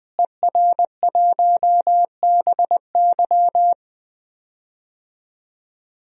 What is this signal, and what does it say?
Morse code, heard as a keyed tone, about 20 wpm, 700 Hz. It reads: ER1BY
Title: DXCC-100 Call Signs - 5 Characters - Repeated 2x - 20wpm